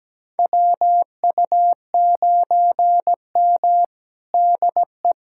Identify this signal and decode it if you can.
Morse code, heard as a keyed tone, about 17 wpm, 700 Hz. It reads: WU9M DE
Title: USA Callsign DE Callsign - 4 Characters - 17wpm